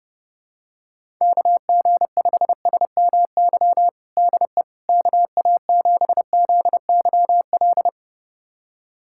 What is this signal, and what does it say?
Morse code, heard as a keyed tone, about 30 wpm, 700 Hz. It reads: KG5SMY DE KA7ZYL
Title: USA Callsign DE Callsign - 6 Characters - 30wpm